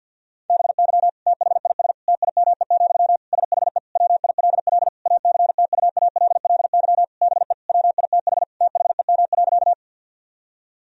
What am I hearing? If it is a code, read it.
Morse code, heard as a keyed tone, about 50 wpm, 700 Hz. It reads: BY THIS TIME, SHE WILL ACTUALLY BE WITH THEM.
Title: Sentences from Top 1000 Words - Review Part 4 - 50wpm